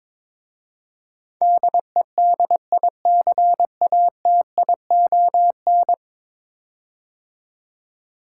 Morse code, heard as a keyed tone, about 22 wpm, 700 Hz. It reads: DEDICATION
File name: practice / 22wpm / 10 Letter Words - First 3 Letters Given - 22wpm